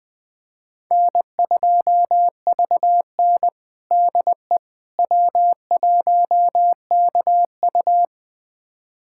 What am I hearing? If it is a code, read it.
Morse code, heard as a keyed tone, about 21 wpm, 700 Hz. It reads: N2VN DE W1KU
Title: USA Callsign DE Callsign 20wpm